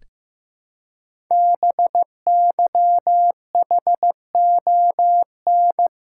Morse code, heard as a keyed tone, about 15 wpm, 700 Hz. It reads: BYHON
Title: Code Groups - 5 Letters - 15wpm